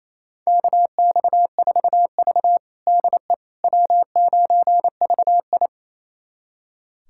Morse code, heard as a keyed tone, about 29 wpm, 700 Hz. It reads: KX4V DE W9VI
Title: USA Callsign DE Callsign - 4 Characters - 28wpm